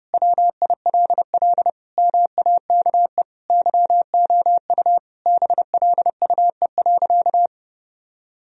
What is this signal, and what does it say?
Morse code, heard as a keyed tone, about 32 wpm, 700 Hz. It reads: WILL MAKE YOU BLUE.
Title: Sentences from Top 1000 Words - Review Part 3 - 30wpm